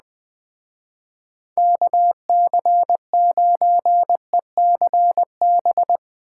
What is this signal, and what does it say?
Morse code, heard as a keyed tone, about 20 wpm, 700 Hz. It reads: KC9ECB
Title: Speed Racing - USA 2x3 Call Signs 17wpm